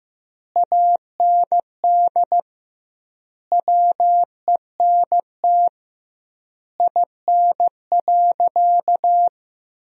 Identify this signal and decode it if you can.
Morse code, heard as a keyed tone, about 16 wpm, 700 Hz. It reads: AND WENT IN.
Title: 2x Word-Spacing - Sentences from Top 300 Words - 15wpm